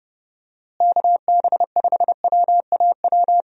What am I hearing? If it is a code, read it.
Morse code, heard as a keyed tone, about 30 wpm, 700 Hz. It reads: KB5WAW
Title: USA Call Signs - 6 Characters - Repeated 3x - 30wpm